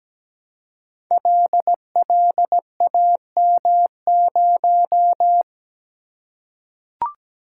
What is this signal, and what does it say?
Morse code, heard as a keyed tone, about 17 wpm, 700 Hz. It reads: LLAM0
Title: Code Groups - 5 Letter:Numbers - 17wpm